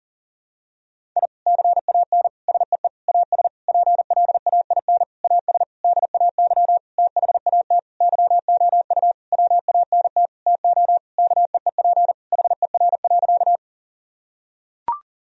Morse code, heard as a keyed tone, about 40 wpm, 700 Hz. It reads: I CAN SEE AS PLAIN AS DAY THAT YOU WANT TO KEEP HER.
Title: Sentences from Top 1000 Words - Review Part 2 - 40wpm